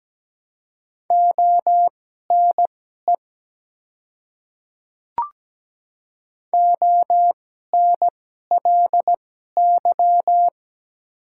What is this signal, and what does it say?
Morse code, heard as a keyed tone, about 17 wpm, spaced as for 12 wpm, 700 Hz. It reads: ONE ONLY
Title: 2x Character Spacing - Top 200 Words - 17wpm